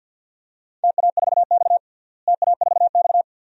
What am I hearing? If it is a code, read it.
Morse code, heard as a keyed tone, about 51 wpm, 700 Hz. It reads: TA4X TA4X
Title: DXCC-100 Call Signs - 4 Characters - Repeated 2x - 50wpm